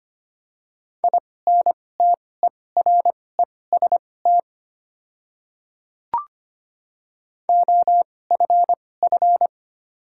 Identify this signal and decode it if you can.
Morse code, heard as a keyed tone, about 25 wpm, spaced as for 18 wpm, 700 Hz. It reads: INTEREST OFF
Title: 2x Character Spacing - Top 500 Words - 25wpm